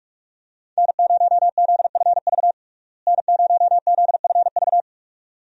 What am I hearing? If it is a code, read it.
Morse code, heard as a keyed tone, about 42 wpm, 700 Hz. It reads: N0ZFV N0ZFV
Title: USA Call Signs - 5 Characters - Repeated 2x - 45wpm